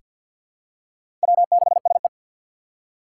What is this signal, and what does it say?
Morse code, heard as a keyed tone, about 46 wpm, 700 Hz. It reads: W6SE
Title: USA Call Signs - 4 Characters - 3x Character Spacing - 50wpm